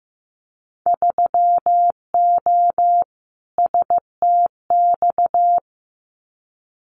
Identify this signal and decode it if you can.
Morse code, heard as a keyed tone, about 15 wpm, 700 Hz. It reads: 3O STX
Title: Winter Field Day Exchanges 15wpm